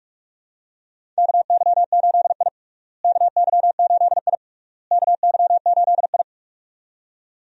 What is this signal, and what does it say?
Morse code, heard as a keyed tone, about 41 wpm, 700 Hz. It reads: KY8I KY8I KY8I
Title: USA Call Signs - 4 Characters - Repeated 3x - 45wpm